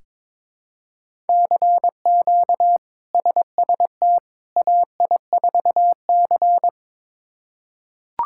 Morse code, heard as a keyed tone, about 21 wpm, 700 Hz. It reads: CQ SST AI4C
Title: K1USN SST QSO Practice - 22wpm